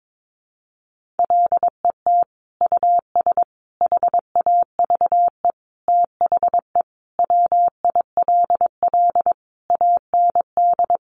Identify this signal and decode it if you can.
Morse code, heard as a keyed tone, about 22 wpm, 700 Hz. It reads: LET US HAVE THE WILL AND
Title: Sentences from Top 1300 Words 22wpm